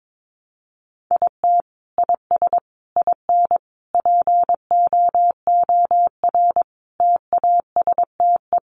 Morse code, heard as a keyed tone, about 22 wpm, 700 Hz. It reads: IT IS IN POOR TASTE
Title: Sentences from Top 1200 Words 22wpm